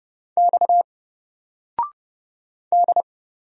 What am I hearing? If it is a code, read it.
Morse code, heard as a keyed tone, about 30 wpm, 700 Hz. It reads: X D
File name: Single-Letter-Number-30wpm